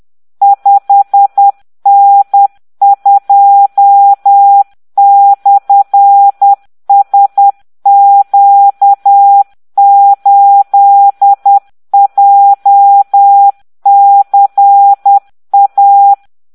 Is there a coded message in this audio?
Yes: Morse code from 0.4 to 16.1 s, heard as a keyed tone, about 10 wpm, 800 Hz. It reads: 5N2/SQ8JCA